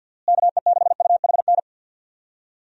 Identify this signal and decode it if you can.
Morse code, heard as a keyed tone, about 50 wpm, 700 Hz. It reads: KE6UHN
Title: USA Call Signs - 6 Characters - Repeated 3x - 50wpm